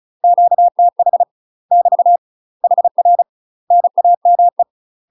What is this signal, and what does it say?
Morse code, heard as a keyed tone, about 36 wpm, 700 Hz. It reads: QTH <BT> HR NAME
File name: UK MASTER QSOs TRAINING / 35WPM / UK MASTER QSOs 0033